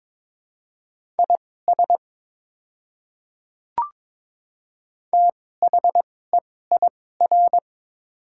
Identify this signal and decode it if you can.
Morse code, heard as a keyed tone, about 22 wpm, spaced as for 16 wpm, 690 Hz. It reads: IS THEIR
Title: 2x Character Spacing - Top 100 Words - 22wpm